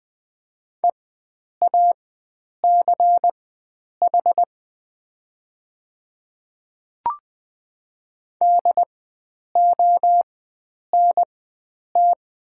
Morse code, heard as a keyed tone, about 20 wpm, spaced as for 9 wpm, 700 Hz. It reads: EACH DONT
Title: 4x Character Spacing - Top 500 Words - 20wpm